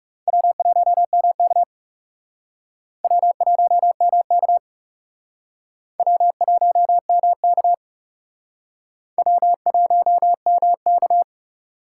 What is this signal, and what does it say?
Morse code, heard as a keyed tone, about 38 wpm, 700 Hz. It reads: W1MK W1MK W1MK W1MK